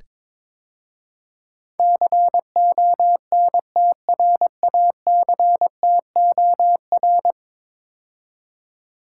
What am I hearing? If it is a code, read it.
Morse code, heard as a keyed tone, about 21 wpm, 700 Hz. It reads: CONTRACTOR
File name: practice / 22wpm / 10 Letter Words - First 3 Letters Given - 22wpm